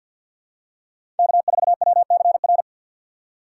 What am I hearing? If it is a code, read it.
Morse code, heard as a keyed tone, about 53 wpm, 700 Hz. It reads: K4WCR
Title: USA Call Signs - 5 Characters - 2x Character Spacing - 50wpm